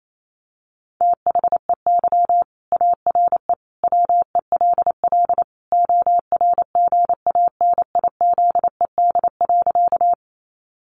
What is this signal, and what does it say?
Morse code, heard as a keyed tone, about 28 wpm, 700 Hz. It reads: THEY ARE WELL ORGANIZED.
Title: Sentences from Top 1300 Words 28wpm